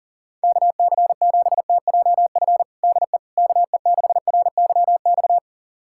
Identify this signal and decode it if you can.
Morse code, heard as a keyed tone, about 40 wpm, 700 Hz. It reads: KC7TJF DE KE6RYX
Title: USA Callsign DE Callsign - 6 Characters - 40wpm